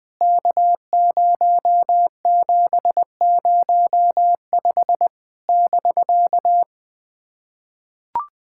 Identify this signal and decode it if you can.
Morse code, heard as a keyed tone, about 20 wpm, 700 Hz. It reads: K0705 <BK>
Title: POTA QSO Practice - 20wpm